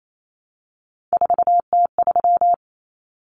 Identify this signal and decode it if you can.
Morse code, heard as a keyed tone, about 28 wpm, 700 Hz. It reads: 4T3
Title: Common Cut Numbers - 3-Digit 28wpm